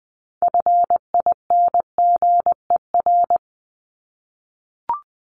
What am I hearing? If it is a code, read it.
Morse code, heard as a keyed tone, about 20 wpm, 700 Hz. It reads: FINGER
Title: Top 1501-1750 Words 20wpm